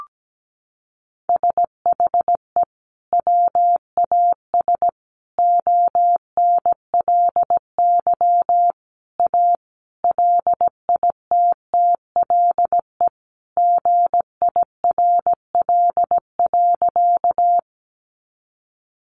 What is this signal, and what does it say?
Morse code, heard as a keyed tone, about 17 wpm, 700 Hz. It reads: SHE WAS ONLY A LITTLE GIRL.
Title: Sentences from Top 400 Words 17wpm